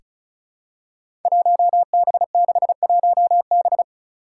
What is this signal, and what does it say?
Morse code, heard as a keyed tone, about 34 wpm, 700 Hz. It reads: 1B61B
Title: Five Challenging Letter-Number Combinations 35wpm